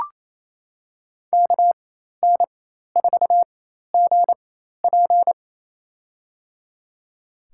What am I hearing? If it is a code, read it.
Morse code, heard as a keyed tone, about 27 wpm, spaced as for 13 wpm, 700 Hz. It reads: KN4GP
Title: USA Call Signs - 4x Character Spacing - 28wpm